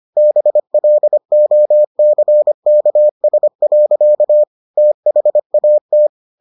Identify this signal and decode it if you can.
Morse code, heard as a keyed tone, about 25 wpm, 600 Hz. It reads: BLOCKS. THAT